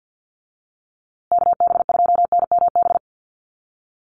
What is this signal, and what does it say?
Morse code, heard as a keyed tone, about 50 wpm, 700 Hz. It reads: KB2NMB
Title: USA Call Signs Repeated 2x 50wpm